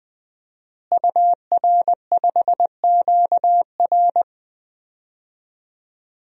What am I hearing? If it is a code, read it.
Morse code, heard as a keyed tone, about 20 wpm, 700 Hz. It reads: UR5QR
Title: DXCC-100 Call Signs - 5 Characters - Repeated 3x - 20wpm